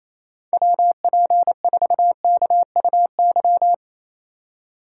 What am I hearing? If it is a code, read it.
Morse code, heard as a keyed tone, about 28 wpm, 700 Hz. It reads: WP4KUY
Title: USA Call Signs - 6 Characters - 3x Character Spacing - 28wpm